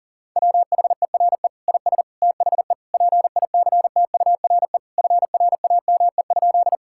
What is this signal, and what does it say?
Morse code, heard as a keyed tone, about 37 wpm, 700 Hz. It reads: WHERE IS THE PICTURE FRAME?
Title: Sentences from Top 1000 Words - Review Part 6 - 40wpm